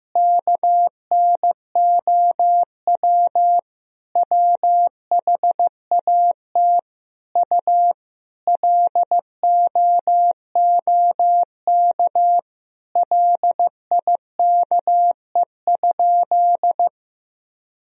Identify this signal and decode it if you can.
Morse code, heard as a keyed tone, about 15 wpm, 700 Hz. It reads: KNOW WHAT U LOOK LIKE?
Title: Sentences from Top 100 Words and Abbreviations 15wpm